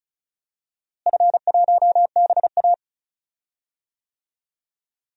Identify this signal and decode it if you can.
Morse code, heard as a keyed tone, about 34 wpm, 700 Hz. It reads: F1BA
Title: DXCC-100 Call Signs - 4 Characters - Repeated 2x - 35wpm